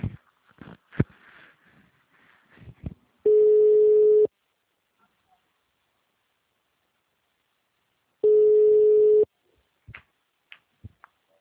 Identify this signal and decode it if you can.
Ringback tone (Europe, 425 Hz).
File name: out- -2048- - - .66954